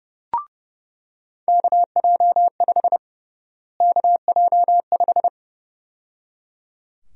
Morse code, heard as a keyed tone, about 29 wpm, 700 Hz. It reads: KJ5 KJ5
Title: USA Call Sign Prefixes - Repeated 2x - 30wpm